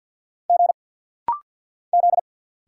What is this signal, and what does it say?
Morse code, heard as a keyed tone, about 50 wpm, 700 Hz. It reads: G Z
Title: Single Letters - Rapid-Fire 50wpm